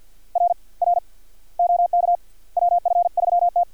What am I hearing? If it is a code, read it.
Morse code, heard as a keyed tone, about 50 wpm, 700 Hz. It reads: R R OK WF3T
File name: archived / CRSnet